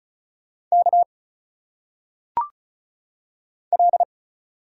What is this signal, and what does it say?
Morse code, heard as a keyed tone, about 34 wpm, 700 Hz. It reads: K L
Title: Single Letters - Phonetic Pronunciation 35wpm